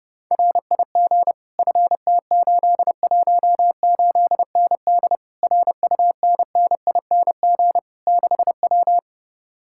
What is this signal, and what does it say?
Morse code, heard as a keyed tone, about 33 wpm, 700 Hz. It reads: RIG FT818ND RUNNING 6W